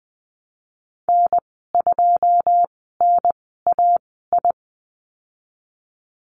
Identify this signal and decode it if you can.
Morse code, heard as a keyed tone, about 20 wpm, spaced as for 15 wpm, 700 Hz. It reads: N2NAI
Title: 088 Lesson 15g - Number 5 - Call Signs 20wpm